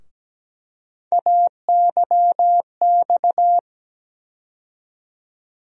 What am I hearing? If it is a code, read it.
Morse code, heard as a keyed tone, about 17 wpm, 700 Hz. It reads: AYX
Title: USA Call Sign Postfixes - 3 Letters 17wpm